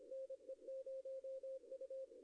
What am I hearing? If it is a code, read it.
Morse code, heard as a keyed tone, about 25 wpm, 530 Hz. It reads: NE0U
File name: contests / arrl-160-k5tr-op-w5tn -0100